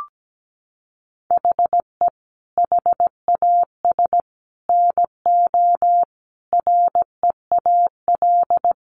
Morse code, heard as a keyed tone, about 17 wpm, 700 Hz. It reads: HE HAS NO REAL
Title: Sentences from Top 1200 Words 17wpm